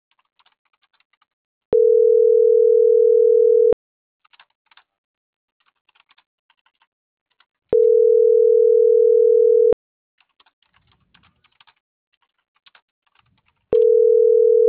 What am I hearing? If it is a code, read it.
Ringback tone (North America), 440 + 480 Hz.